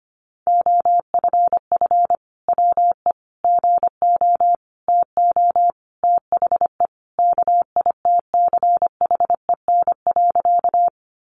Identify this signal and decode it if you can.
Morse code, heard as a keyed tone, about 25 wpm, 700 Hz. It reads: OFF WE GO TO THE KITCHEN.
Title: Sentences from Top 1400 Words 25wpm